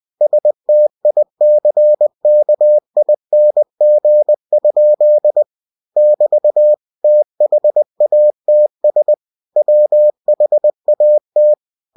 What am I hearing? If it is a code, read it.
Morse code, heard as a keyed tone, about 20 wpm, 600 Hz. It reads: STICKING? <BT> THATS WHAT
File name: cw-books / winnie-the-pooh / winnie-20wpm-page-0085